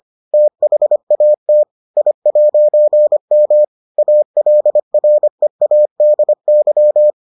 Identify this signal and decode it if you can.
Morse code, heard as a keyed tone, about 25 wpm, 600 Hz. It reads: THAT I'M ALREADY